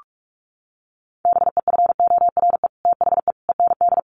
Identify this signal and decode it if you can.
Morse code, heard as a keyed tone, about 46 wpm, 700 Hz. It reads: BEFORE THE END
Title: Sets of 5 Words Encore 45wpm